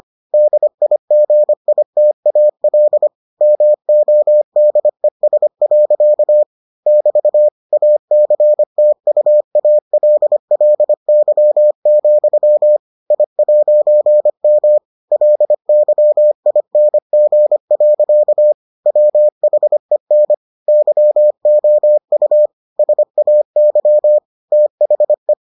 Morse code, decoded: DIGITAL MODES. <BT> ACTUALLY, I'M LYING. WHEN YOU SAY THE

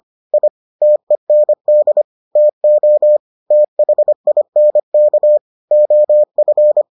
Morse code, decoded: I TEND TO THINK OF

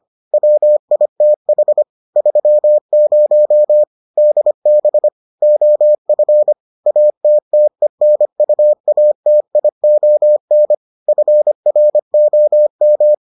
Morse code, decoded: WITH 30 DB OF ATTENUATION FROM

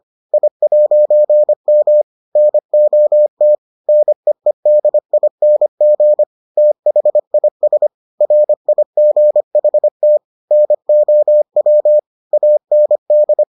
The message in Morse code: I'M NOT NEEDING THIS RIGHT NOW AND

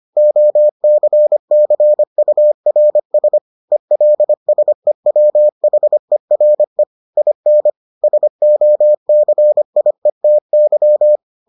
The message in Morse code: OCCURS ELSEWHERE IN SOCIETY

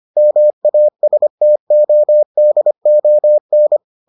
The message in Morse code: MASTODON

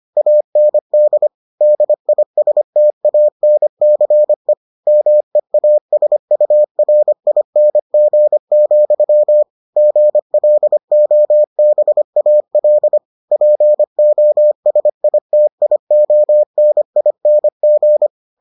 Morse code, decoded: AND DISTANCE MEASURING, GLOBAL POSITIONING

